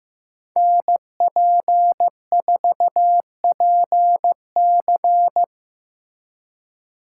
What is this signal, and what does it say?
Morse code, heard as a keyed tone, about 18 wpm, 700 Hz. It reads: NP4PC
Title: Speed Racing - USA 2x2 Call Signs 15wpm